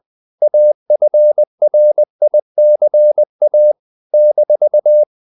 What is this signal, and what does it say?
Morse code, heard as a keyed tone, about 20 wpm, 600 Hz. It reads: AFRICA -
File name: CW / Practice / NationalCapitals20